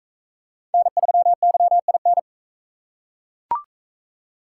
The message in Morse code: N3YIN

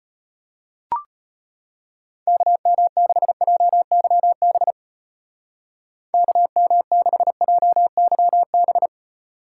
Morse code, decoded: KM6JYB KM6JYB

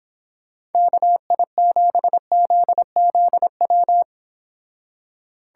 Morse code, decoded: KI7ZZW

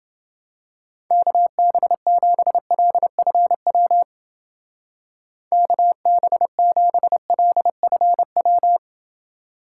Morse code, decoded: KB7LFW KB7LFW

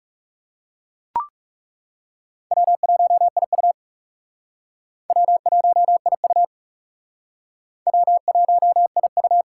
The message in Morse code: W1IU W1IU W1IU

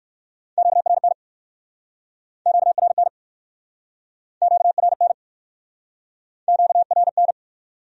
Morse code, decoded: QRN QRN QRN QRN

